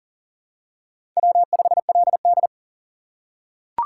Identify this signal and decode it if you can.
Morse code, heard as a keyed tone, about 40 wpm, 700 Hz. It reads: W5LD
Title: USA Call Signs - 4 Characters - 2x Character Spacing - 40wpm